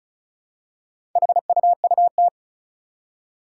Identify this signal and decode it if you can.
Morse code, heard as a keyed tone, about 36 wpm, 700 Hz. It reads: HUUT
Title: Four Challenging Letter-Number Combinations 35wpm